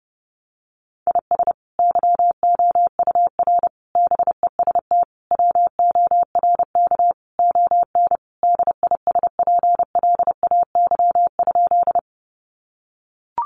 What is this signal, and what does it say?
Morse code, heard as a keyed tone, about 30 wpm, 700 Hz. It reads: IS YOUR BEST WORK ON DISPLAY?
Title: Sentences from Top 1200 Words 30wpm